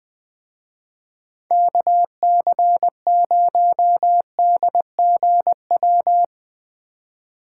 Morse code, heard as a keyed tone, about 20 wpm, 700 Hz. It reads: KC0DGW